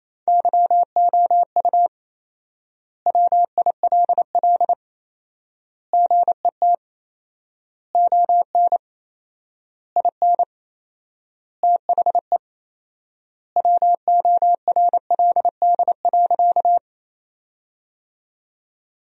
Morse code, heard as a keyed tone, about 28 wpm, 700 Hz. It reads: YOU WILL GET ON IN THE WORLD.
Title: 4x Word-Spacing - Sentences from Top 200 Words - 28wpm